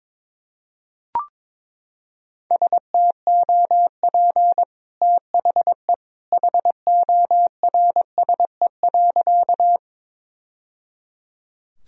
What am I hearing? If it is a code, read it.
Morse code, heard as a keyed tone, about 22 wpm, 700 Hz. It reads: STOP THE HORSE.
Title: Sentences from Top 1000 Words - Review Part 1 - 22wpm